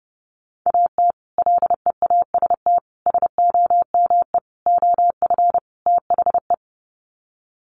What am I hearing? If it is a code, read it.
Morse code, heard as a keyed tone, about 30 wpm, 700 Hz. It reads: AT LEAST SOME OF THE
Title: Sets of 5 Words Encore 30wpm